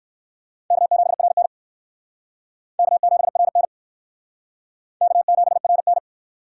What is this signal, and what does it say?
Morse code, heard as a keyed tone, about 56 wpm, 700 Hz. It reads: K7RN K7RN K7RN
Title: Speed Racing - USA 1x2 Call Signs 45wpm